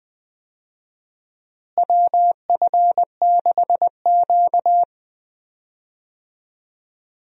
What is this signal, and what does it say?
Morse code, heard as a keyed tone, about 20 wpm, 700 Hz. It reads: WF6Q
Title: USA Call Signs - 4 Characters - Repeated 3x - 20wpm